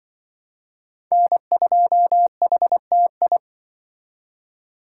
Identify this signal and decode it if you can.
Morse code, heard as a keyed tone, about 24 wpm, 700 Hz. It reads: N2HTI
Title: Speed Racing - USA Call Signs 20wpm